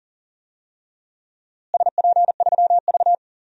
Morse code, heard as a keyed tone, about 38 wpm, 700 Hz. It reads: SP3V
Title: DXCC-100 Call Signs - 4 Characters - Repeated 3x - 40wpm